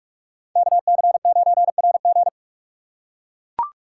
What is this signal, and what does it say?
Morse code, heard as a keyed tone, about 41 wpm, 700 Hz. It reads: KC9RG